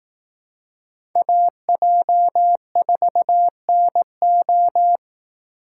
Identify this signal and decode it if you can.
Morse code, heard as a keyed tone, about 18 wpm, 700 Hz. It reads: AJ4NO
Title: Speed Racing - USA 2x2 Call Signs 15wpm